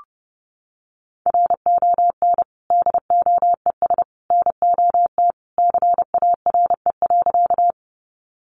Morse code, heard as a keyed tone, about 30 wpm, 700 Hz. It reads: RON DOES NOT CARE.
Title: Additional Easy Phrases 30wpm